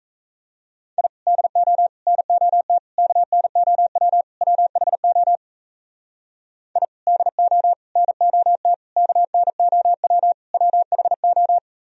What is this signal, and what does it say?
Morse code, heard as a keyed tone, about 37 wpm, 700 Hz. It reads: I DO NOT KNOW WHO I DO NOT KNOW WHO